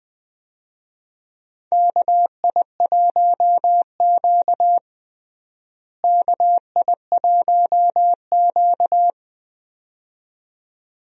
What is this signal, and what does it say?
Morse code, heard as a keyed tone, about 20 wpm, 700 Hz. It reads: KI1Q KI1Q
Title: USA Call Signs - 4 Characters - Repeated 2x - 20wpm